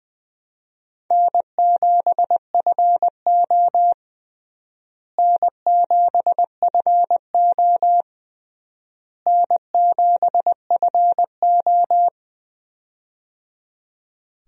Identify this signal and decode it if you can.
Morse code, heard as a keyed tone, about 20 wpm, 700 Hz. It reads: N7FO N7FO N7FO
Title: USA Call Signs - 4 Characters - Repeated 3x - 20wpm